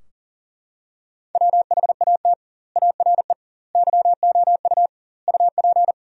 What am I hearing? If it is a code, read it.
Morse code, heard as a keyed tone, about 40 wpm, 700 Hz. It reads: WHAT ARE YOU UP
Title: 2x Word-Spacing - Sentences from Top 100 Words - 40wpm